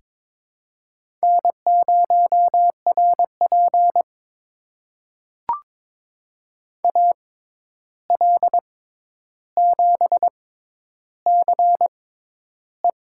Morse code, heard as a keyed tone, about 23 wpm, spaced as for 11 wpm, 700 Hz. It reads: N0RP AL7CE
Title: USA Call Signs - 6x Character Spacing - 22wpm